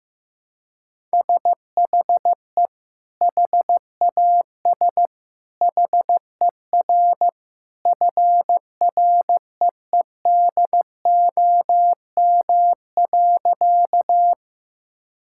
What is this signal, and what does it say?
Morse code, heard as a keyed tone, about 16 wpm, 700 Hz. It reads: SHE HAS HER FREEDOM.
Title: Sentences from Top 1000 Words - Review Part 5 - 15wpm